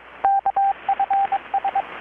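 Morse code, heard as a keyed tone, about 22 wpm, 790 Hz, with radio noise. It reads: KFS
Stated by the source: Morse from KPH's Night of Nights XX maritime radiotelegraphy event, received in Chiapas, Mexico about 3,800 km away on a Kenwood R-600 with a grounded 10 m long-wire antenna.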